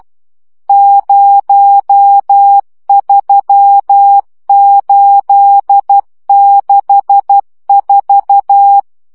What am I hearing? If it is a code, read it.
Morse code, heard as a keyed tone, about 12 wpm, 800 Hz. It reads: 03864
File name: morze vizsga 2011-2 10